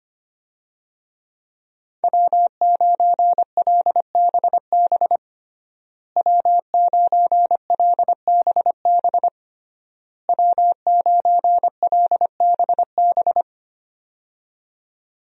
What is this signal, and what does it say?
Morse code, heard as a keyed tone, about 25 wpm, 700 Hz. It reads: W9LBB W9LBB W9LBB
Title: USA Call Signs - 5 Characters - Repeated 3x - 25wpm